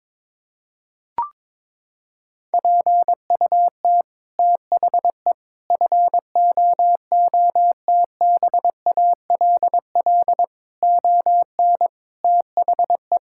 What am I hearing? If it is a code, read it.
Morse code, heard as a keyed tone, about 22 wpm, 700 Hz. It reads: PUT THE FOOTBALL ON THE
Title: Sentences from Top 1600 Words 22wpm